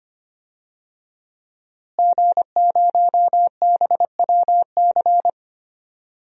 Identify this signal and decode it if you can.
Morse code, heard as a keyed tone, about 25 wpm, 700 Hz. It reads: G0BWC
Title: DXCC-100 Call Signs - 5 Characters - Repeated 3x - 25wpm